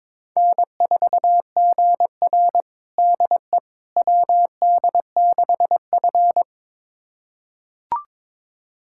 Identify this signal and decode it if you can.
Morse code, heard as a keyed tone, about 22 wpm, 700 Hz. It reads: N4GR DE WD6F
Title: USA Callsign DE Callsign - 4 Characters - 22wpm